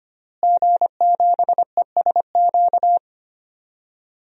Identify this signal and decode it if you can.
Morse code, heard as a keyed tone, about 25 wpm, 700 Hz. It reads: G7ESQ
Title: DXCC-100 Call Signs - 5 Characters - Repeated 3x - 25wpm